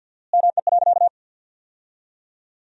Morse code, heard as a keyed tone, about 53 wpm, 700 Hz. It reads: ME.
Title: 4x Word-Spacing - Sentences from Top 200 Words - 50wpm